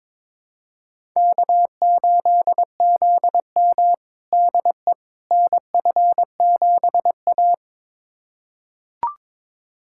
Morse code, heard as a keyed tone, about 22 wpm, 700 Hz. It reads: K8ZM DE NF7A
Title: USA Callsign DE Callsign - 4 Characters - 22wpm